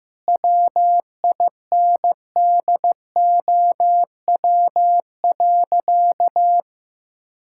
Morse code, heard as a keyed tone, about 16 wpm, 700 Hz. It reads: WINDOW.
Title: Sentences from Top 1600 Words 15wpm